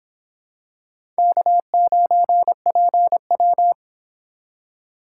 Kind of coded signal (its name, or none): Morse code